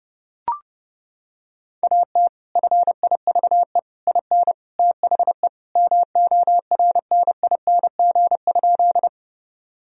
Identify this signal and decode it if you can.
Morse code, heard as a keyed tone, about 32 wpm, 700 Hz. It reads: AT FIVE IN THE MORNING?
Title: Sentences from Top 1000 Words - Review Part 2 - 30wpm